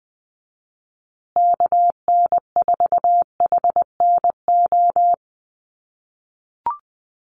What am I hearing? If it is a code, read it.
Morse code, heard as a keyed tone, about 20 wpm, 700 Hz. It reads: KN4HNO
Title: USA Call Signs Repeated 3x 20wpm